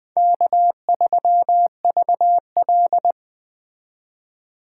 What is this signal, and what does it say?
Morse code, heard as a keyed tone, about 20 wpm, 700 Hz. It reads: K3VL
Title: Speed Racing - USA 1x2 Call Signs 17wpm